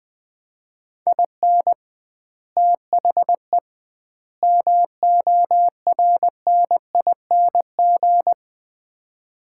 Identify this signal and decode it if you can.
Morse code, heard as a keyed tone, about 20 wpm, 700 Hz. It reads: IN THE MORNING
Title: 2x Word-Spacing - Sets of 3 Words - 20wpm